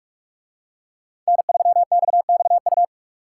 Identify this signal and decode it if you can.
Morse code, heard as a keyed tone, about 44 wpm, 700 Hz. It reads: N3XXU